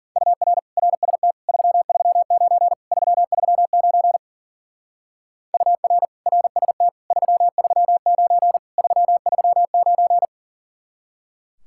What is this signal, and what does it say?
Morse code, heard as a keyed tone, about 48 wpm, 700 Hz. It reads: UR RST 339 339 UR RST 339 339
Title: Speed-Racing Common QSO Phrases - 40wpm